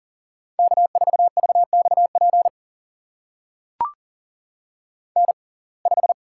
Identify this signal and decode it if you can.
Morse code, heard as a keyed tone, about 40 wpm, spaced as for 29 wpm, 700 Hz. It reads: K4VXP N5
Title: USA Call Signs - 5 Characters - 6x Character Spacing - 40wpm